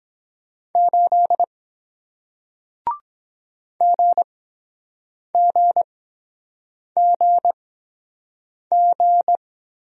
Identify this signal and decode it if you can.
Morse code, heard as a keyed tone, about 23 wpm, spaced as for 7 wpm, 700 Hz. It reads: 8 GGGG